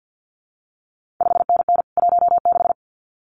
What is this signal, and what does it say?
Morse code, heard as a keyed tone, about 50 wpm, 700 Hz. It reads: 5NN 16
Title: CQ WW DX Contest Exchange - 50wpm